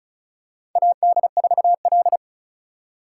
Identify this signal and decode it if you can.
Morse code, heard as a keyed tone, about 34 wpm, 700 Hz. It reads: AD4L
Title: USA Call Signs - 3x Character Spacing - 35wpm